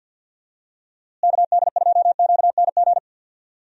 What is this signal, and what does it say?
Morse code, heard as a keyed tone, about 52 wpm, 700 Hz. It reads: KD2QNG
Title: USA Call Signs - 6 Characters - Repeated 3x - 50wpm